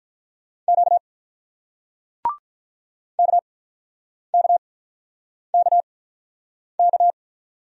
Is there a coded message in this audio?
Yes: Morse code from 0.7 to 7.1 s, heard as a keyed tone, about 48 wpm, spaced as for 9 wpm, 700 Hz. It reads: Q KKKK